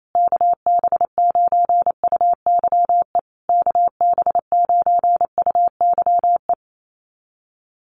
Morse code, heard as a keyed tone, about 28 wpm, 700 Hz. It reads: KB9UYE KB9UYE
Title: USA Call Signs Repeated 3x 28wpm